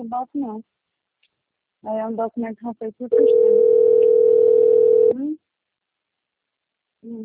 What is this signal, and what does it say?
Ringback tone (North America), 440 + 480 Hz, with speech.
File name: out- - - - - .61637